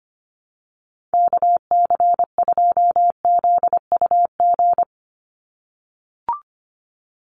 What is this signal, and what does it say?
Morse code, heard as a keyed tone, about 25 wpm, 700 Hz. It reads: KC2ZUG
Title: USA Call Signs 25wpm